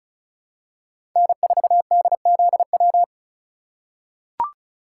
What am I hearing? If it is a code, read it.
Morse code, heard as a keyed tone, about 36 wpm, 700 Hz. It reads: N4DZW
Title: USA Call Signs - 5 Characters - 3x Character Spacing - 35wpm